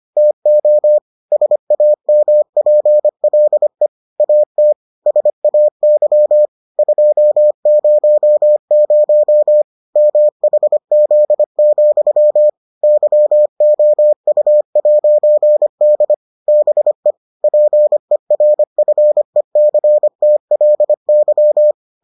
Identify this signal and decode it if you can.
Morse code, heard as a keyed tone, about 25 wpm, 600 Hz. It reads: TO SAMPLE AT SAY 200 MHZ, YOU'D BE PERFECTLY